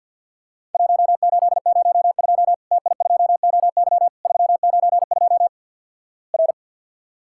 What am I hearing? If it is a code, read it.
Morse code, heard as a keyed tone, about 50 wpm, 700 Hz. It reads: 1802 TI2OY 392 R
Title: Worked All Europe DX Contest - QTC Exchange - 50wpm